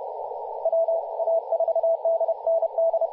Morse code, heard as a keyed tone, about 31 wpm, 660 Hz, with radio noise. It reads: WA4DND
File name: posts / 2016-july-18-2021z-t30ar-14-021mhz-cw